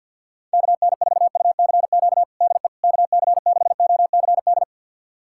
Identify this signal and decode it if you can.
Morse code, heard as a keyed tone, about 52 wpm, 700 Hz. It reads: KN4UCQ DE KC6OCD
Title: USA Callsign DE Callsign - 6 Characters - 50wpm